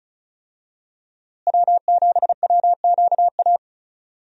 Morse code, heard as a keyed tone, about 36 wpm, 700 Hz. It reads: W7WQA
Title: USA Call Signs - 5 Characters - Repeated 3x - 35wpm